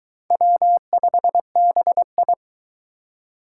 Morse code, heard as a keyed tone, about 22 wpm, 700 Hz. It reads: W5BI